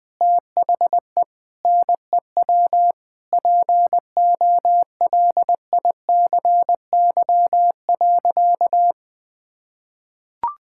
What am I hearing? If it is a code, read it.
Morse code, heard as a keyed tone, about 20 wpm, 700 Hz. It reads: THE NEW POLICY.
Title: Sentences with 8-Word Phrases - Spoken Beginning - 20wpm